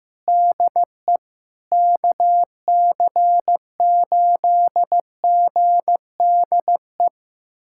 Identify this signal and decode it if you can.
Morse code, heard as a keyed tone, about 15 wpm, 700 Hz. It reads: DE KC8GDE
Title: USA Callsign DE Callsign - 6 Characters - 15wpm